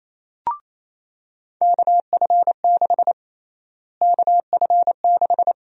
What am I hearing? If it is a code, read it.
Morse code, heard as a keyed tone, about 28 wpm, 700 Hz. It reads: KF6 KF6
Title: USA Call Sign Prefixes - Repeated 2x - 28wpm